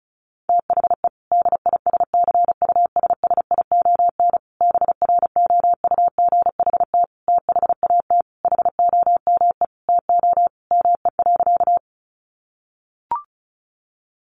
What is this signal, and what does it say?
Morse code, heard as a keyed tone, about 34 wpm, 700 Hz. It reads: THE DISCUSSION BROUGHT THAT HOME TO ME.
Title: Sentences from Top 1100 Words 35wpm